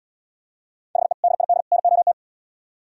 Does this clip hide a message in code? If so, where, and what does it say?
Yes: Morse code from 1.0 to 2.1 s, heard as a keyed tone, about 76 wpm, 700 Hz. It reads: HE DID NOT